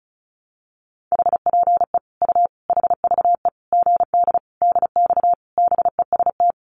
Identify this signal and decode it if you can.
Morse code, heard as a keyed tone, about 34 wpm, 700 Hz. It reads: HPE U HVE GD DX BEST
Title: Common QSO Phrases 35wpm